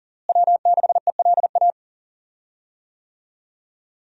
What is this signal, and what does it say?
Morse code, heard as a keyed tone, about 40 wpm, 700 Hz. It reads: W6ELA
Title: USA Call Signs - 5 Characters - Repeated 3x - 40wpm